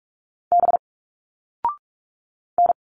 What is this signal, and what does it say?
Morse code, heard as a keyed tone, about 46 wpm, 700 Hz. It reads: B N
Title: Single Letters 45wpm